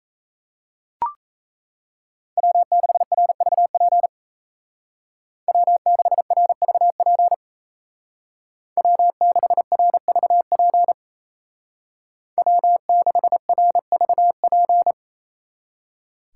Morse code, heard as a keyed tone, about 33 wpm, 700 Hz. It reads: W6RVP W6RVP W6RVP W6RVP
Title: Speed Racing - USA Call Signs 28wpm